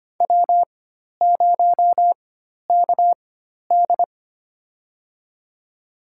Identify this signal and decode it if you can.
Morse code, heard as a keyed tone, about 25 wpm, spaced as for 12 wpm, 700 Hz. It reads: W0KD
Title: USA Call Signs - 4 Characters - 4x Character Spacing - 25wpm